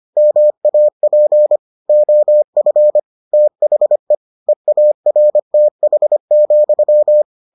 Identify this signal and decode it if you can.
Morse code, heard as a keyed tone, about 25 wpm, 600 Hz. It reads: MAP OF THE EARTH,